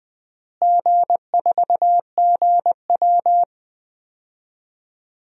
Morse code, heard as a keyed tone, about 20 wpm, 700 Hz. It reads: G4GW
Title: DXCC-100 Call Signs - 4 Characters - Repeated 3x - 20wpm